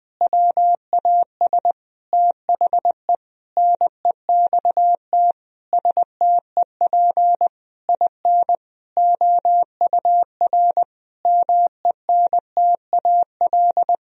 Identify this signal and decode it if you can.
Morse code, heard as a keyed tone, about 20 wpm, 700 Hz. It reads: WAS THE NEXT STEP IN OUR MENTAL
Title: Sentences from Top 1000 Words - Review Part 4 - 20wpm